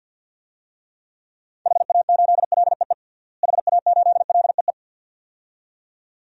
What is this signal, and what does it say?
Morse code, heard as a keyed tone, about 52 wpm, 700 Hz. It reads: HA8LEE HA8LEE
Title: DXCC-100 Call Signs - 6 Characters - Repeated 2x - 50wpm